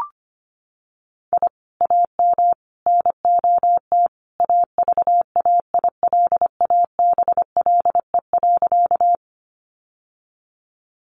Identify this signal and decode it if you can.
Morse code, heard as a keyed tone, about 25 wpm, 700 Hz. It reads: I AM NOT AVAILABLE.
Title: Sentences from Top 600 Words 25wpm